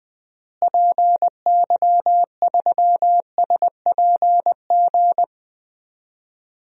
Morse code, heard as a keyed tone, about 20 wpm, 700 Hz. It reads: PY3SPG
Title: DXCC-100 Call Signs - 6 Characters - Repeated 2x - 20wpm